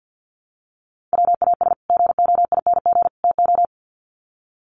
Morse code, heard as a keyed tone, about 50 wpm, 700 Hz. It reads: WAS GOING TO
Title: Sets of 3 Words 50wpm